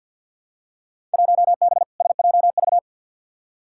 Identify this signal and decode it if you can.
Morse code, heard as a keyed tone, about 52 wpm, 700 Hz. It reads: 1B SJV
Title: Field Day Exchanges Encore - 50wpm